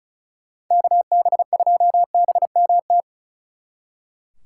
Morse code, heard as a keyed tone, about 44 wpm, 700 Hz. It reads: KB2BMT
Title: Speed Racing - USA 2x3 Call Signs 35wpm